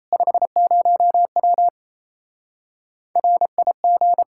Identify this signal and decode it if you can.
Morse code, heard as a keyed tone, about 38 wpm, 700 Hz. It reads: 50W RIG